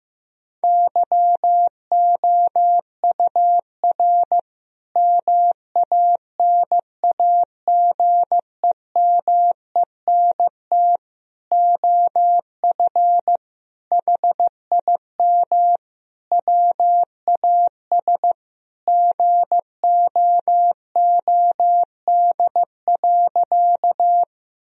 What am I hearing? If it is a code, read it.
Morse code, heard as a keyed tone, about 16 wpm, 700 Hz. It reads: YOUR MANAGEMENT OF HIM WAS GOOD.
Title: Sentences from Top 1000 Words - Review Part 4 - 15wpm